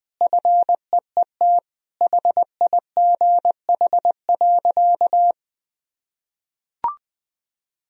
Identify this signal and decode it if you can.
Morse code, heard as a keyed tone, about 20 wpm, 700 Hz. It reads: FEET HIGH.
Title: Sentences from Top 1000 Words - Review Part 2 - 20wpm